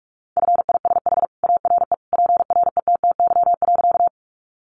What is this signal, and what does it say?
Morse code, heard as a keyed tone, about 44 wpm, 700 Hz. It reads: FISH ARE PRETTY.
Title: Additional Easy Phrases 45wpm